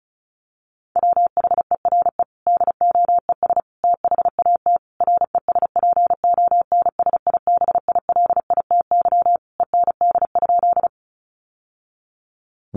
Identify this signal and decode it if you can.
Morse code, heard as a keyed tone, about 34 wpm, 700 Hz. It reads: WHERE DOES THAT RESPONSIBILITY END?
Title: Sentences from Top 1200 Words 35wpm